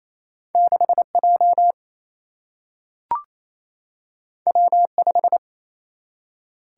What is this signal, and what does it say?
Morse code, heard as a keyed tone, about 28 wpm, 700 Hz. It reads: 6J W5
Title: Single-Letter Single-Number 28wpm